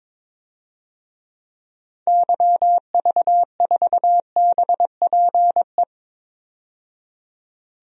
Morse code, heard as a keyed tone, about 22 wpm, 700 Hz. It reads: YV4BPE
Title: DXCC-100 Call Signs - 6 Characters - 22wpm